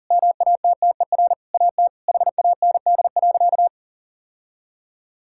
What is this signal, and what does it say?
Morse code, heard as a keyed tone, about 40 wpm, 700 Hz. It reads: MATTER AT HAND.
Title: Sentences from Top 1600 Words 40wpm